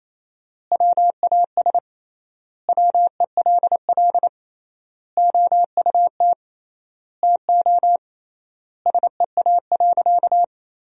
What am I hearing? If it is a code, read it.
Morse code, heard as a keyed tone, about 28 wpm, 700 Hz. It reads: WAS WELL OUT TO SEA.
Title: 3x Word-Spacing - Sentences from Top 400 Words - 28wpm